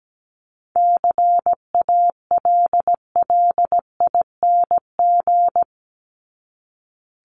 Morse code, heard as a keyed tone, about 17 wpm, 700 Hz. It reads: CALLING
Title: ING Suffix - Top 100 Words with Base Words 17wpm